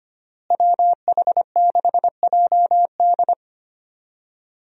Morse code, heard as a keyed tone, about 25 wpm, 700 Hz. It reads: WH6JD
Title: USA Call Signs - 5 Characters - Repeated 3x - 25wpm